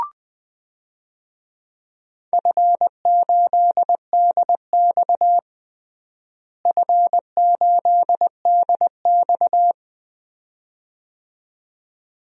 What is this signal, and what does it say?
Morse code, heard as a keyed tone, about 20 wpm, 700 Hz. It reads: F8DX F8DX
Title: DXCC-100 Call Signs - 4 Characters - Repeated 2x - 20wpm